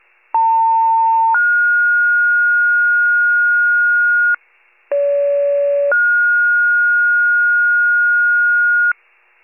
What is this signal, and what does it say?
Two-tone page, heard as a one-second tone, then a three-second tone, about 900 then 1430 Hz, about 570 then 1430 Hz, with noise. It is heard from 0.3 to 8.9 s.
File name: pager / 29 FIRE / wcalert